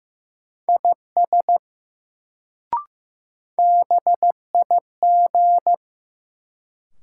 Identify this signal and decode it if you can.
Morse code, heard as a keyed tone, about 15 wpm, 700 Hz. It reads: IS BIG
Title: Sets of 2 Words - New Word Drill - 15wpm